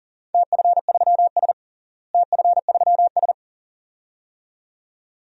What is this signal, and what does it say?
Morse code, heard as a keyed tone, about 37 wpm, 700 Hz. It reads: TF3S TF3S
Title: DXCC-100 Call Signs - 4 Characters - Repeated 2x - 40wpm